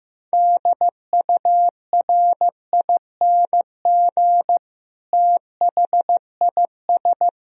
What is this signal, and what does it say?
Morse code, heard as a keyed tone, about 16 wpm, 700 Hz. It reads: DURING THIS
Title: Sentences from Top 1000 Words - Review Part 2 - 15wpm